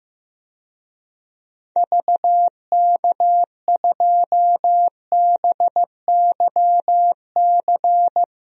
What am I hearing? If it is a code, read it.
Morse code, heard as a keyed tone, about 15 wpm, 700 Hz. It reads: VK2BYC
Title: DXCC-100 Call Signs - 6 Characters - Repeated 3x - 15wpm